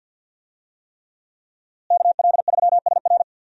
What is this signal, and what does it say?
Morse code, heard as a keyed tone, about 50 wpm, 700 Hz. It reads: KL3SR
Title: USA Call Signs - 5 Characters - Repeated 3x - 50wpm